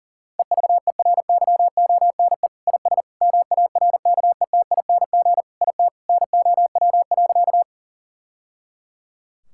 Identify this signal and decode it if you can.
Morse code, heard as a keyed tone, about 40 wpm, 700 Hz. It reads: EVERYONE IS MARKETING IT NOW.
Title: Sentences from Top 1000 Words - Review Part 5 - 40wpm